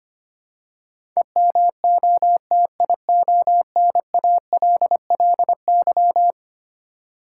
Morse code, heard as a keyed tone, about 25 wpm, 700 Hz. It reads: EMOTIONALLY
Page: morse-code-ninja-practice-25wpm